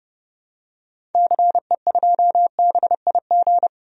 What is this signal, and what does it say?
Morse code, heard as a keyed tone, about 30 wpm, 700 Hz. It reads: CE2BIG
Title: DXCC-100 Call Signs - 6 Characters - 30wpm